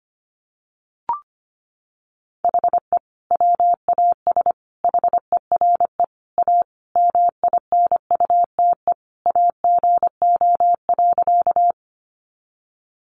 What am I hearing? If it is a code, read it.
Morse code, heard as a keyed tone, about 25 wpm, 700 Hz. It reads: HE WAS HERE A MINUTE AGO.
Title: Sentences from Top 500 Words 25wpm